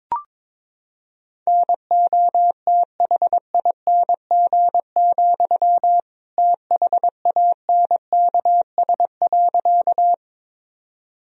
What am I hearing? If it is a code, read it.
Morse code, heard as a keyed tone, about 22 wpm, 700 Hz. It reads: NOTHING, THANKS.
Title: Sentences from Top 1000 Words - Review Part 3 - 22wpm